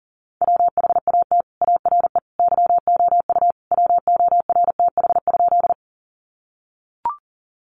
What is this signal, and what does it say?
Morse code, heard as a keyed tone, about 40 wpm, 700 Hz. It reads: WHAT ARE YOU WORTH?
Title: Sentences from Top 800 Words 40wpm